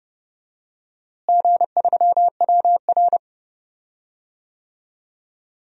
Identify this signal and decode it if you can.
Morse code, heard as a keyed tone, about 30 wpm, 700 Hz. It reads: G3WR
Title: DXCC-100 Call Signs - 4 Characters - Repeated 3x - 30wpm